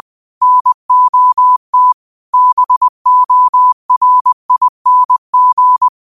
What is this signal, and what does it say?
Morse code, heard as a keyed tone, about 20 wpm, 1000 Hz. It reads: NOT BORING